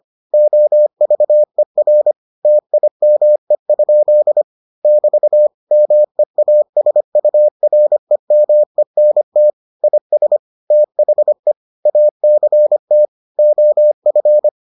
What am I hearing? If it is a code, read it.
Morse code, heard as a keyed tone, about 25 wpm, 600 Hz. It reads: OVER TIME? <BT> MEASUREMENT IS THE ACT OF